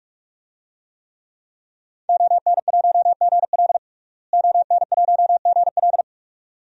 Morse code, heard as a keyed tone, about 43 wpm, 700 Hz. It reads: ON1GL ON1GL
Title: DXCC-100 Call Signs - 5 Characters - Repeated 2x - 45wpm